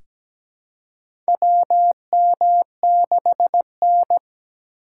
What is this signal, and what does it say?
Morse code, heard as a keyed tone, about 17 wpm, 700 Hz. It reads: WM6N
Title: USA Call Signs - 4 Characters - 2x Character Spacing - 17wpm